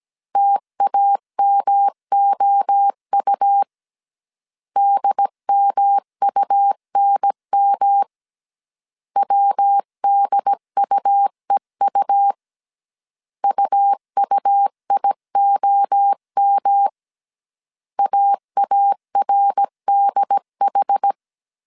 Morse code, heard as a keyed tone, about 17 wpm, 800 Hz. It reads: TAMOU DMUNM WDUEU UUIOM AARDH
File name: CW / DET / 1 3 1500C S A 16wpm 2141